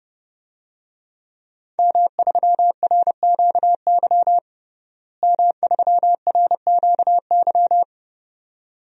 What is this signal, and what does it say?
Morse code, heard as a keyed tone, about 30 wpm, 700 Hz. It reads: M3RQY M3RQY
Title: DXCC-100 Call Signs - 5 Characters - Repeated 2x - 30wpm